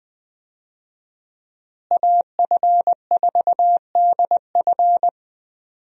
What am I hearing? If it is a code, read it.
Morse code, heard as a keyed tone, about 20 wpm, 700 Hz. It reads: AF4DF
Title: USA Call Signs - 5 Characters - Repeated 2x - 20wpm